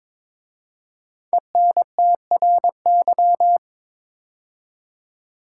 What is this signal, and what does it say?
Morse code, heard as a keyed tone, about 22 wpm, 700 Hz. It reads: ENTRY
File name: 5 Letter Words - First 3 Letters Given - 22wpm